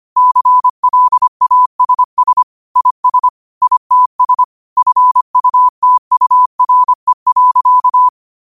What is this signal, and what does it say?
Morse code, heard as a keyed tone, about 25 wpm, 1000 Hz. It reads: CLASS IS ITS FUTURE.
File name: qotdfiles / 25wpm